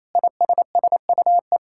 Morse code, heard as a keyed tone, about 28 wpm, 700 Hz. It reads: ISSUE
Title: Sentences from Top 1000 Words - Review Part 6 - 28wpm